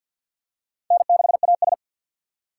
Morse code, heard as a keyed tone, about 46 wpm, 700 Hz. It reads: N6AS
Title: USA Call Signs - 4 Characters - 3x Character Spacing - 50wpm